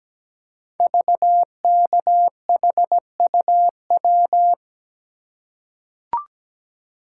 Morse code, heard as a keyed tone, about 17 wpm, 700 Hz. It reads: VKHUW
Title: Code Groups - 5 Letters - 17wpm